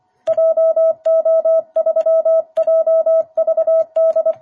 Morse code, heard as a keyed tone, about 25 wpm, 660 Hz. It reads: JO3JVD
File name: jh8cbh / 2021cw